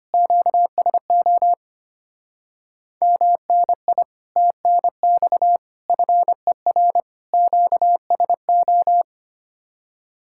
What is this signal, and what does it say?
Morse code, heard as a keyed tone, about 32 wpm, 700 Hz. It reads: QSO MNI TNX FER QSO